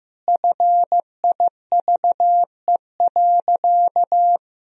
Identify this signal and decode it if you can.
Morse code, heard as a keyed tone, about 15 wpm, 700 Hz. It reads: FIVE.
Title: Sentences from Top 1000 Words - Review Part 3 - 15wpm